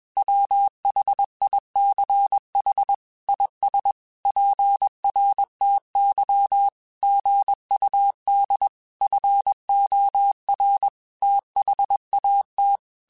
Morse code, heard as a keyed tone, about 21 wpm, 800 Hz. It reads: WHICH IS PRTY GUD FOR THAT
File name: QSOs / QSOs-mp3s / WA7QOX 1@20